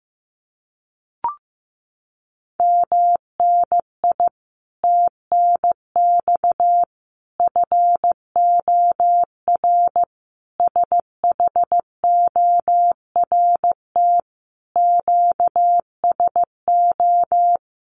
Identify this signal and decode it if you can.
Morse code, heard as a keyed tone, about 15 wpm, 700 Hz. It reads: MNI TNX FOR SHORT QSO
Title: Common QSO Phrases 15wpm